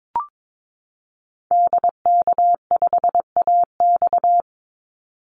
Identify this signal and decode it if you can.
Morse code, heard as a keyed tone, about 22 wpm, 700 Hz. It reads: DK5AX
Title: Call Signs Repeated 2x - 22wpm